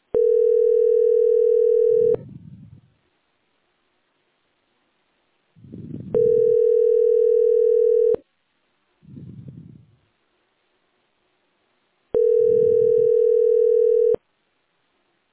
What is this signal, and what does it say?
Ringback tone (North America), 440 + 480 Hz.
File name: 9c896d41-5495-4931-bf80